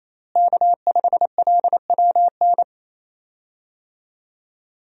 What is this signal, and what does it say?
Morse code, heard as a keyed tone, about 28 wpm, 700 Hz. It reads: K5LWN
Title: USA Call Signs - 5 Characters - Repeated 3x - 28wpm